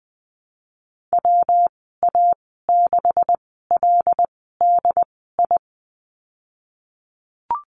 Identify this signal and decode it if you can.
Morse code, heard as a keyed tone, about 20 wpm, spaced as for 15 wpm, 700 Hz. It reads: WA6LDI
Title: 167 Lesson 27g - Number 7 - Call Signs 20wpm